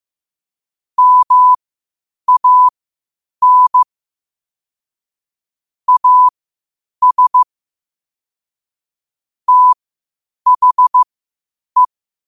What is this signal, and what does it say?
Morse code, heard as a keyed tone, about 15 wpm, spaced as for 8 wpm, 1000 Hz. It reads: MAN AS THE
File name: qotdfiles / 8wpm